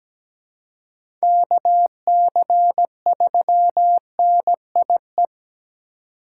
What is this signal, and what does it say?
Morse code, heard as a keyed tone, about 17 wpm, 700 Hz. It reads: KC3NIE
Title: K1USN SST QSO Practice - 17wpm